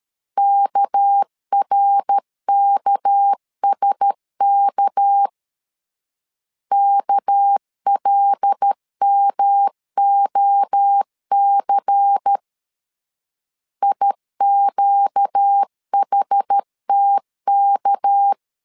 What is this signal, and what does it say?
Morse code, heard as a keyed tone, about 13 wpm, 800 Hz. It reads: KRKSK KLMOC IQHTK
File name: LEZIONI / 6 PKX0 / 6 PK 12wpm 2006